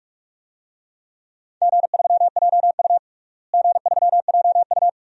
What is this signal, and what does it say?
Morse code, heard as a keyed tone, about 42 wpm, 700 Hz. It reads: G3JU G3JU
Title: DXCC-100 Call Signs - 4 Characters - Repeated 2x - 45wpm